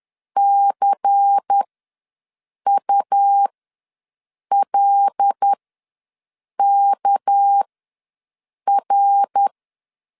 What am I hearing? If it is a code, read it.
Morse code, heard as a keyed tone, about 10 wpm, spaced as for 6 wpm, 800 Hz. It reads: CULKR
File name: LEZIONI / 6 PKX0 / 6 PK ws 10wpm 3745